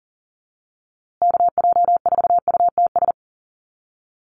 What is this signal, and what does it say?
Morse code, heard as a keyed tone, about 40 wpm, 700 Hz. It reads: KJ4UTS
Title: USA Call Signs Repeated 3x 40wpm